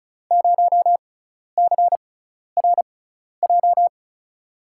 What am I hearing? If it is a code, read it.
Morse code, heard as a keyed tone, about 36 wpm, spaced as for 12 wpm, 700 Hz. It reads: 0CRJ
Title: USA Call Signs - 5 Characters - 6x Character Spacing - 35wpm